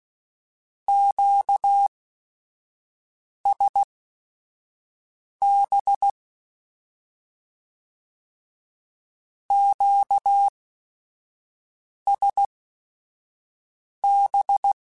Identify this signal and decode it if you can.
Morse code, heard as a keyed tone, about 16 wpm, spaced as for 5 wpm, 800 Hz. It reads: QSB QSB
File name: Q Calls / Q Calls-mp3s / Q Calls Twice @5